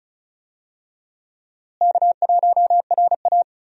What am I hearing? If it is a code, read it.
Morse code, heard as a keyed tone, about 34 wpm, 700 Hz. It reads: K1RA
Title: USA Call Signs - 4 Characters - Repeated 2x - 35wpm